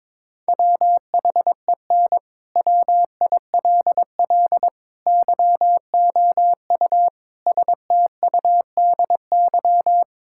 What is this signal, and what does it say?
Morse code, heard as a keyed tone, about 22 wpm, 700 Hz. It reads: WHEN WILL YOU STUDY
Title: Sentences from Top 1000 Words - Review Part 6 - 22wpm